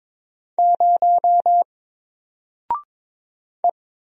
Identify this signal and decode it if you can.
Morse code, heard as a keyed tone, about 22 wpm, 700 Hz. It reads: 0 E
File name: Single-Letter-Number-22wpm